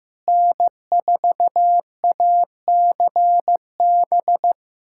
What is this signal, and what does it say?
Morse code, heard as a keyed tone, about 15 wpm, 700 Hz. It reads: N4ACB
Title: USA Callsign DE Callsign 15wpm